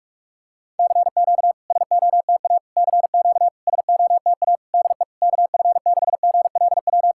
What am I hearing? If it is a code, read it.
Morse code, heard as a keyed tone, about 47 wpm, 690 Hz. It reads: CQ SOTA CQ SOTA DE KF6GLW
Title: SOTA QSO Practice 45wpm